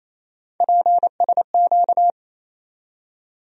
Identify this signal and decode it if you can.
Morse code, heard as a keyed tone, about 28 wpm, 700 Hz. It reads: PSQ
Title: USA Call Sign Postfixes - 3 Letters 28wpm